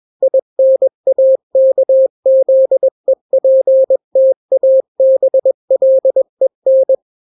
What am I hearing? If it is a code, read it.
Morse code, heard as a keyed tone, about 21 wpm, 530 Hz. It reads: INAKZEPTABLEN
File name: daily / 20wpm